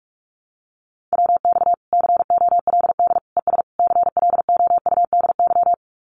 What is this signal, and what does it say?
Morse code, heard as a keyed tone, about 44 wpm, 700 Hz. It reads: WX COLD ES CLOUDY
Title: Common QSO Phrases 45wpm